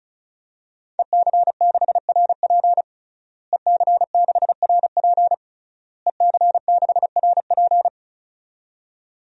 Morse code, heard as a keyed tone, about 36 wpm, 700 Hz. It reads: EC6RP EC6RP EC6RP
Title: DXCC-100 Call Signs - 5 Characters - Repeated 3x - 35wpm